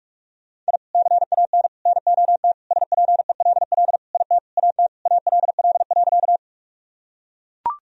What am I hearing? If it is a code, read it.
Morse code, heard as a keyed tone, about 44 wpm, 700 Hz. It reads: I CAN NOT SPELL IT AT ALL.
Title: Sentences from Top 1000 Words - Review Part 1 - 45wpm